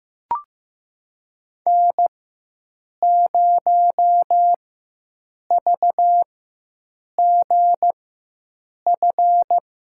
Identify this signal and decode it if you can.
Morse code, heard as a keyed tone, about 15 wpm, spaced as for 7 wpm, 700 Hz. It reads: N0VGF
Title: USA Call Signs - 4x Character Spacing - 15wpm